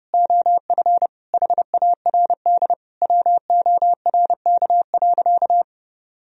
Morse code, heard as a keyed tone, about 30 wpm, 700 Hz. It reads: OF HARD WORK.
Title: Sentences from Top 1000 Words - Review Part 2 - 30wpm